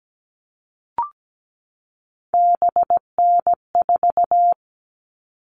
Morse code, heard as a keyed tone, about 17 wpm, 700 Hz. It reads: BN4
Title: All Cut Numbers - 3-Digit 17wpm